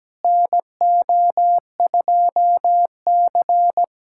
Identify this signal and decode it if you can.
Morse code, heard as a keyed tone, about 17 wpm, 700 Hz. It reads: NO2C
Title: USA Callsign DE Callsign - 4 Characters - 17wpm